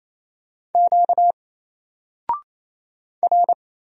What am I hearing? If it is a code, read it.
Morse code, heard as a keyed tone, about 27 wpm, 700 Hz. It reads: Q R
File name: Single-Letter-Number-28wpm